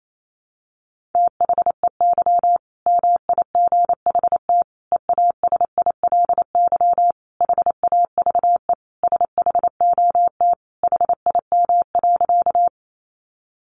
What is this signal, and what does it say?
Morse code, heard as a keyed tone, about 28 wpm, 700 Hz. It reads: THEY MIGHT EASILY HAVE SHOT HIM.
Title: Sentences from Top 1100 Words 28wpm